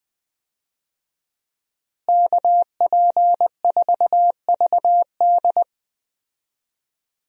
Morse code, heard as a keyed tone, about 20 wpm, 700 Hz. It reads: KP4VD
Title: DXCC-100 Call Signs - 5 Characters - Repeated 3x - 20wpm